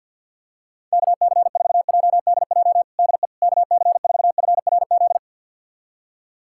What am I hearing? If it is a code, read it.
Morse code, heard as a keyed tone, about 51 wpm, 700 Hz. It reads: KC4JDJ DE KC4FRZ
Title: USA Callsign DE Callsign - 6 Characters - 50wpm